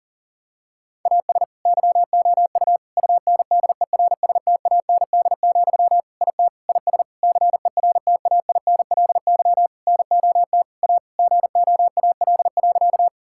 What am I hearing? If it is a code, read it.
Morse code, heard as a keyed tone, about 40 wpm, 700 Hz. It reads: AS YOU UNDERSTAND, IT IS CERTAINLY NOT A GOAL.
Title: Sentences from Top 1000 Words - Review Part 5 - 40wpm